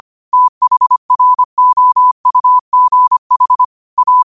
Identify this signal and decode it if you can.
Morse code, heard as a keyed tone, about 25 wpm, 1000 Hz. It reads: THROUGH A